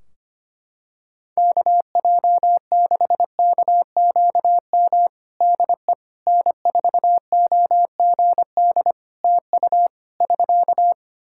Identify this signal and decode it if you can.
Morse code, heard as a keyed tone, about 25 wpm, 700 Hz. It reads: KJ6KQM DE N4OGD TU <SK>
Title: Practice QSO - Exchange 4 - Ending - 25wpm